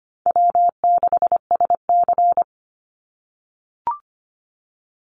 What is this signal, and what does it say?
Morse code, heard as a keyed tone, about 25 wpm, 700 Hz. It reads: W6SC
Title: Easy Call Signs - 25wpm